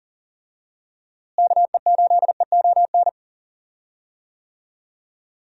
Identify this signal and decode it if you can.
Morse code, heard as a keyed tone, about 37 wpm, 700 Hz. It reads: KE8EON
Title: USA Call Signs - 6 Characters - Repeated 2x - 40wpm